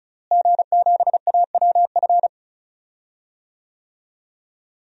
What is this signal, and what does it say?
Morse code, heard as a keyed tone, about 36 wpm, 700 Hz. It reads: G7AWF
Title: DXCC-100 Call Signs - 5 Characters - Repeated 2x - 35wpm